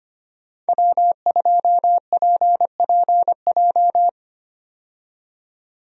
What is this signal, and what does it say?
Morse code, heard as a keyed tone, about 25 wpm, 700 Hz. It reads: W2PPJ
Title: USA Call Signs - 5 Characters - 25wpm